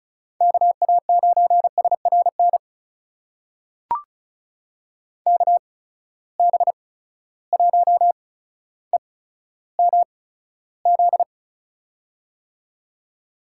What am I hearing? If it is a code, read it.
Morse code, heard as a keyed tone, about 36 wpm, spaced as for 15 wpm, 700 Hz. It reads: KA9SRN KB1EMZ
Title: USA Call Signs - 6 Characters - 8x Character Spacing - 35wpm